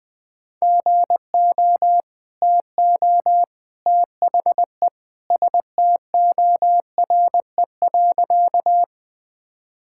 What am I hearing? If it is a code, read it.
Morse code, heard as a keyed tone, about 20 wpm, 700 Hz. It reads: GO TO THE STORE.
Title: Sentences with 2-Word Phrases - Spoken Beginning - 20wpm